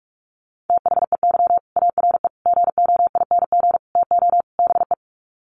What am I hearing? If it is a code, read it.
Morse code, heard as a keyed tone, about 44 wpm, 690 Hz. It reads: THEY ARE GOING TO BE
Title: Sets of 5 Words Encore2 45wpm